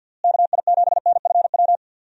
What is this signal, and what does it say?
Morse code, heard as a keyed tone, about 51 wpm, 700 Hz. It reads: KI7NFW
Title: USA Call Signs - 6 Characters - Repeated 2x - 50wpm